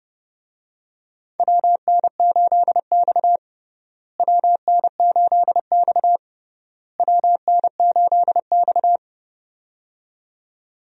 Morse code, heard as a keyed tone, about 32 wpm, 700 Hz. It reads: WN8X WN8X WN8X
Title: USA Call Signs - 4 Characters - Repeated 3x - 30wpm